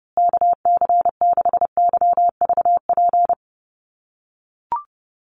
Morse code, heard as a keyed tone, about 30 wpm, 700 Hz. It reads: KC6YVP
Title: USA Call Signs Repeated 3x 30wpm